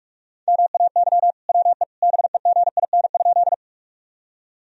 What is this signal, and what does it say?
Morse code, heard as a keyed tone, about 46 wpm, 700 Hz. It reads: MAY WE BEGIN?
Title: Sentences from Top 1000 Words - Review Part 1 - 45wpm